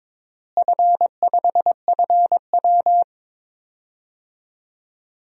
Morse code, heard as a keyed tone, about 22 wpm, 700 Hz. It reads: F5FW
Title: DXCC-100 Call Signs - 4 Characters - Repeated 3x - 22wpm